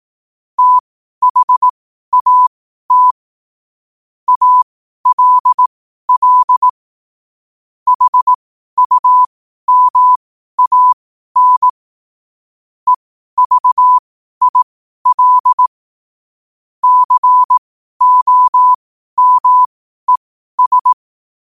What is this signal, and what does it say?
Morse code, heard as a keyed tone, about 18 wpm, spaced as for 12 wpm, 1000 Hz. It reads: THAT ALL HUMAN EVIL COMES
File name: qotdfiles / 12wpm